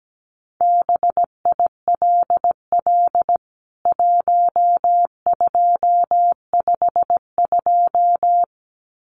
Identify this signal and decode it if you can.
Morse code, heard as a keyed tone, about 17 wpm, 700 Hz. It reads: BILL 1252
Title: CWT Exchange 17wpm